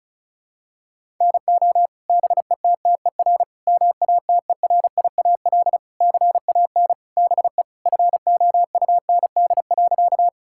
Morse code, heard as a keyed tone, about 36 wpm, 690 Hz. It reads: NO BETTER MATERIAL CAN BE FOUND.
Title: Sentences from Top 1000 Words - Review Part 5 - 35wpm